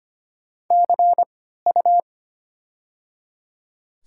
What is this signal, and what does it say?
Morse code, heard as a keyed tone, about 25 wpm, 700 Hz. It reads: C U
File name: Top 90 QSO Elements - 3x Character Spacing - 25wpm